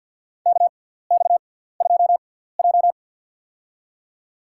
Morse code, heard as a keyed tone, about 46 wpm, spaced as for 17 wpm, 700 Hz. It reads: KX2J
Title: USA Call Signs - 4 Characters - 6x Character Spacing - 50wpm